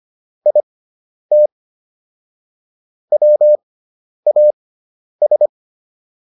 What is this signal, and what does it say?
Morse code, heard as a keyed tone, about 25 wpm, spaced as for 10 wpm, 600 Hz. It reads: IT WAS
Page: intermediate-practice-files